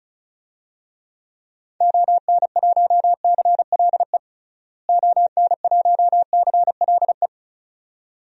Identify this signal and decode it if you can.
Morse code, heard as a keyed tone, about 34 wpm, 700 Hz. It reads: ON1CLE ON1CLE
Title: DXCC-100 Call Signs - 6 Characters - Repeated 2x - 35wpm